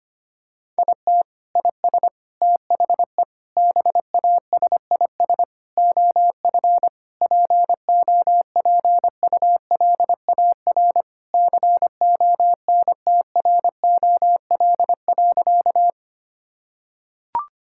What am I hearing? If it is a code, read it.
Morse code, heard as a keyed tone, about 25 wpm, 700 Hz. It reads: IT IS THE BASIS OF POPULAR CONTROL.
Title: Sentences from Top 1000 Words - Review Part 5 - 25wpm